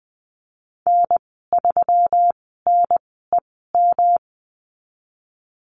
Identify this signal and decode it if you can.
Morse code, heard as a keyed tone, about 19 wpm, spaced as for 14 wpm, 700 Hz. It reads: N3NEM
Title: 128 Lesson 21g - Number 6 - Call Signs 20wpm